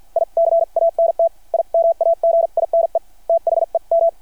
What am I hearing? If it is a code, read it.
Morse code, heard as a keyed tone, about 51 wpm, spaced as for 43 wpm, 660 Hz. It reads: I CANT IMAGINE THEM
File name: archived / CRSnet